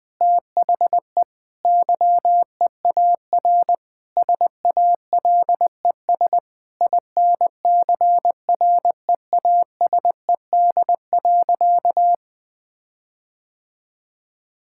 Morse code, heard as a keyed tone, about 21 wpm, 700 Hz. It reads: THE YEAR SALES INCREASED.
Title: Sentences with 6-Word Phrases - 20wpm